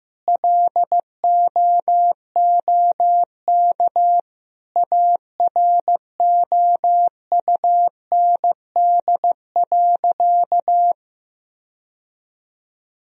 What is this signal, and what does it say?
Morse code, heard as a keyed tone, about 15 wpm, 700 Hz. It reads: LOOK AROUND.
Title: Sentences from Top 1000 Words - Review Part 3 - 15wpm